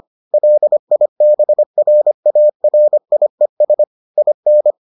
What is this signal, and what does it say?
Morse code, heard as a keyed tone, about 25 wpm, 600 Hz. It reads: LIBRARIES IN